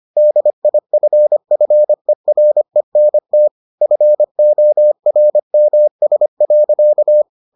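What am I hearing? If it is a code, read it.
Morse code, heard as a keyed tone, about 25 wpm, 600 Hz. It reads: DIFFERENT FORMS.
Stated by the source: Morse edition of the Foundations of Amateur Radio episodes.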